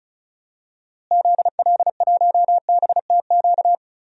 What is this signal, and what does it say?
Morse code, heard as a keyed tone, about 36 wpm, 700 Hz. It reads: ZL1BTQ
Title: DXCC-100 Call Signs - 6 Characters - Repeated 2x - 35wpm